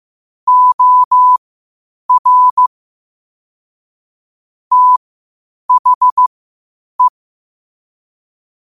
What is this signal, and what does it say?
Morse code, heard as a keyed tone, about 15 wpm, spaced as for 8 wpm, 1000 Hz. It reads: OR THE
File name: qotdfiles / 8wpm